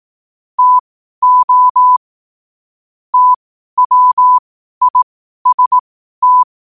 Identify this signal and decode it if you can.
Morse code, heard as a keyed tone, about 18 wpm, spaced as for 12 wpm, 1000 Hz. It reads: TO TWIST
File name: qotdfiles / 12wpm